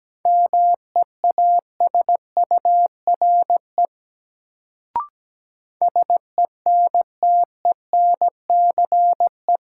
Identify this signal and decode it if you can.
Morse code, heard as a keyed tone, about 17 wpm, 700 Hz. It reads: MEASURE SENTENCE
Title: Sentences from Top 1000 Words - Review Part 1 - New Word Drill - 17wpm